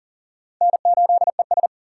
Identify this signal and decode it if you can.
Morse code, heard as a keyed tone, about 38 wpm, 700 Hz. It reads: N8ES
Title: USA Call Signs - 4 Characters - 6x Character Spacing - 40wpm